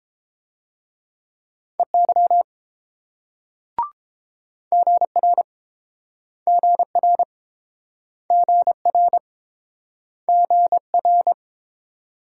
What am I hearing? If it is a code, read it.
Morse code, heard as a keyed tone, about 31 wpm, 700 Hz. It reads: EY GR GR GR GR